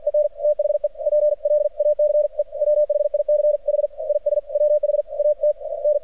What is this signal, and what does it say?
Morse code, heard as a keyed tone, about 53 wpm, 600 Hz. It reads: EN THE PLACE WHICH IS WHAT N
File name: archived / CRSnet